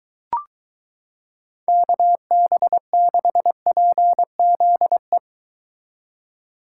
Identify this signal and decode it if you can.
Morse code, heard as a keyed tone, about 23 wpm, 700 Hz. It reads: KB6PZE